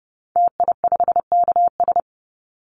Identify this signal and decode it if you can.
Morse code, heard as a keyed tone, about 30 wpm, 700 Hz. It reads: TI5KS
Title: Call Signs Repeated 2x - 30wpm